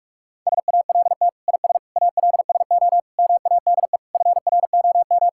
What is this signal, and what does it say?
Morse code, heard as a keyed tone, about 43 wpm, 700 Hz. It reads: SALT IS ALSO MADE FROM
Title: Sentences from Top 1000 Words - Review Part 5 - 45wpm